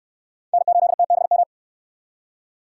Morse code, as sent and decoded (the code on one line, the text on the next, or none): -. ---.. - -... --
N8TBM